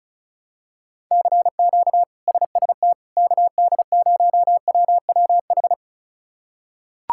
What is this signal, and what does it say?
Morse code, heard as a keyed tone, about 34 wpm, 700 Hz. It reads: CQ SST KD0WWH
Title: K1USN SST QSO Practice - 35wpm